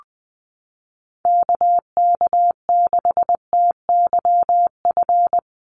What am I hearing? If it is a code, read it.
Morse code, heard as a keyed tone, about 20 wpm, 700 Hz. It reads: KK6TYF
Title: USA Call Signs Repeated 2x 20wpm